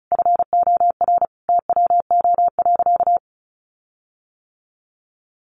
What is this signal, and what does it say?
Morse code, heard as a keyed tone, about 34 wpm, 700 Hz. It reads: FOR TWO.
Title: Sentences from Top 1400 Words 35wpm